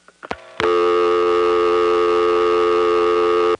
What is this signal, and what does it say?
Dial tone (North America), 350 + 440 Hz, with noise.
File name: phonetrips / EvanNX11HQ